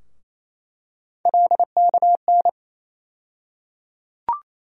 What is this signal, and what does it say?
Morse code, heard as a keyed tone, about 28 wpm, 700 Hz. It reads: LKN
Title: USA Call Sign Postfixes - 3 Letters 28wpm